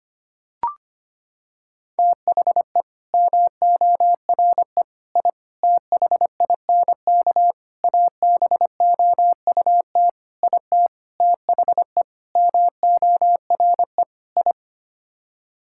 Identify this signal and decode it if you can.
Morse code, heard as a keyed tone, about 25 wpm, 700 Hz. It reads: THE MORE I THINK ABOUT IT THE MORE I
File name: Sets of 9 Words - 25wpm